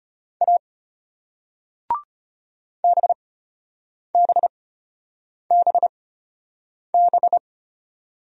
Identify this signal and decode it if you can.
Morse code, heard as a keyed tone, about 33 wpm, spaced as for 8 wpm, 700 Hz. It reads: A BBBB